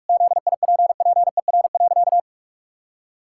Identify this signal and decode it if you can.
Morse code, heard as a keyed tone, about 46 wpm, 700 Hz. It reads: ZIPPER.
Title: Pangram Challenge v2 - 45wpm